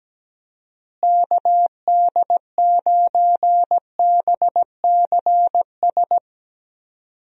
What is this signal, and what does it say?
Morse code, heard as a keyed tone, about 17 wpm, 700 Hz. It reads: KD9BCS
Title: USA Call Signs - 6 Characters - Repeated 3x - 17wpm